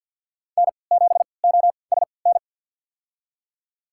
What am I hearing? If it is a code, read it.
Morse code, heard as a keyed tone, about 52 wpm, spaced as for 29 wpm, 700 Hz. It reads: N7OSN
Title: USA Call Signs - 5 Characters - 3x Character Spacing - 50wpm